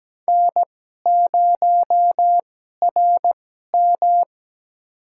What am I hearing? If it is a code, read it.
Morse code, heard as a keyed tone, about 17 wpm, spaced as for 12 wpm, 700 Hz. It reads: N0RM
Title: USA Call Signs - 4 Characters - 2x Character Spacing - 17wpm